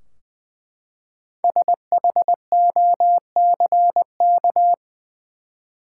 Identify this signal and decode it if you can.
Morse code, heard as a keyed tone, about 20 wpm, 700 Hz. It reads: SHOCK
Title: Five Letter Words - 20wpm